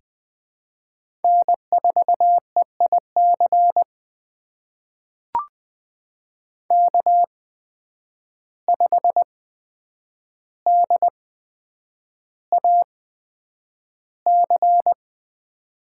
Morse code, heard as a keyed tone, about 20 wpm, spaced as for 9 wpm, 700 Hz. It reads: N4EIC K5DAC
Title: USA Call Signs - 5 Characters - 8x Character Spacing - 20wpm